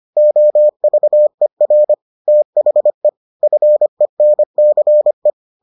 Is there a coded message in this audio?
Yes: Morse code from 0.2 to 5.3 s, heard as a keyed tone, about 25 wpm, 600 Hz. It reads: OVER THE FENCE